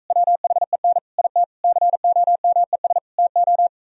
Morse code, heard as a keyed tone, about 45 wpm, 700 Hz. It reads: WHEN IT COMES TO